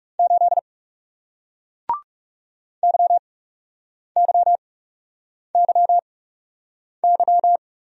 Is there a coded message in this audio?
Yes: Morse code from 0.2 to 7.6 s, heard as a keyed tone, about 40 wpm, spaced as for 8 wpm, 700 Hz. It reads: 8 YYYY